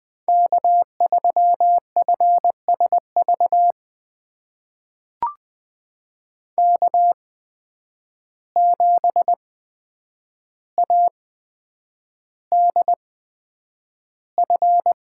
Morse code, heard as a keyed tone, about 20 wpm, spaced as for 9 wpm, 700 Hz. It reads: K3FSV K7ADF
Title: USA Call Signs - 5 Characters - 8x Character Spacing - 20wpm